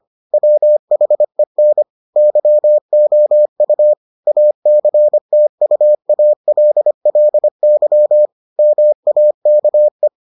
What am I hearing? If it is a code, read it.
Morse code, heard as a keyed tone, about 25 wpm, 600 Hz. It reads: WHEN YOU ACTUALLY MAKE